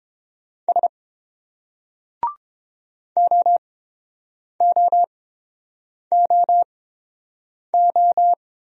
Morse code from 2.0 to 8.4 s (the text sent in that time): OOOO